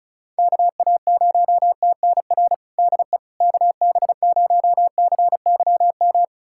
Morse code, heard as a keyed tone, about 36 wpm, 700 Hz. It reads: KA0TNR DE KB0CYM
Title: USA Callsign DE Callsign 35wpm